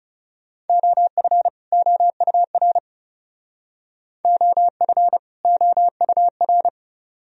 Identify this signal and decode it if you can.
Morse code, heard as a keyed tone, about 37 wpm, 700 Hz. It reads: OF OUR OF OUR
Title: Speed-Racing - Sets of 2 Words 30wpm